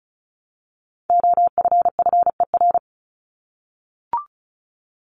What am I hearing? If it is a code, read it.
Morse code, heard as a keyed tone, about 34 wpm, 700 Hz. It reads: OFFER
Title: ED Suffix - Top 100 Words with Base Words 35wpm